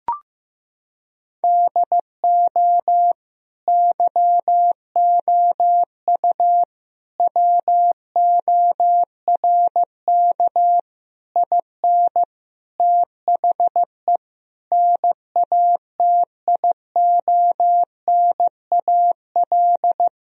Morse code, heard as a keyed tone, about 15 wpm, 700 Hz. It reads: DO YOU WORK IN THE NATIONAL
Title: Sentences from Top 1000 Words - Review Part 4 - 15wpm